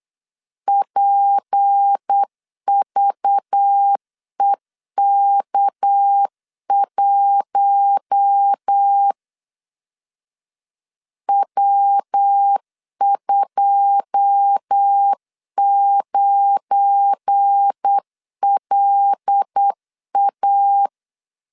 Morse code, heard as a keyed tone, about 8 wpm, 800 Hz. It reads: PVEK1 W29LA
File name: LEZIONI / 9 2 8wpm 3437